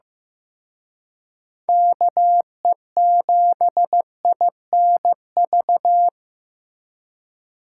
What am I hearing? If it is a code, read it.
Morse code, heard as a keyed tone, about 18 wpm, 700 Hz. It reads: KE7INV
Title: Speed Racing - USA 2x3 Call Signs 15wpm